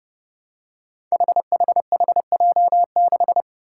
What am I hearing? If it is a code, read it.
Morse code, heard as a keyed tone, about 30 wpm, 700 Hz. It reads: HHHJ6
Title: Five Challenging Letter-Number Combinations 30wpm